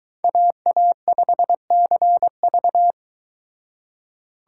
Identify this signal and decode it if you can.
Morse code, heard as a keyed tone, about 22 wpm, 700 Hz. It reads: AA5CV